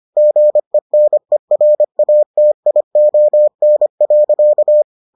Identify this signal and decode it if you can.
Morse code, heard as a keyed tone, about 25 wpm, 600 Hz. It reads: GENERATION.